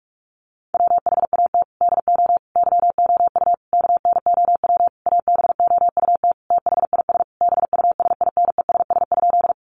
Morse code, heard as a keyed tone, about 44 wpm, 700 Hz. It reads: WHAT DO YOU KNOW ABOUT THIS BUSINESS?
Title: Sentences from Top 600 Words 45wpm